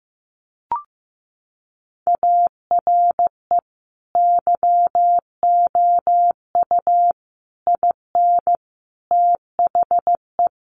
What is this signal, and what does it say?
Morse code, heard as a keyed tone, about 15 wpm, 700 Hz. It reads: ARE YOU IN THE
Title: Sentences from Top 1200 Words 15wpm